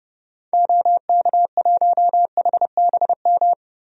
Morse code, heard as a keyed tone, about 30 wpm, 700 Hz. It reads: OK1HBM
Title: DXCC-100 Call Signs - 6 Characters - Repeated 2x - 30wpm